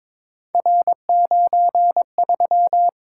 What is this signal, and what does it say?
Morse code, heard as a keyed tone, about 22 wpm, 700 Hz. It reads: R93
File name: practice / 22wpm / Three Letter-Number Combinations - 22wpm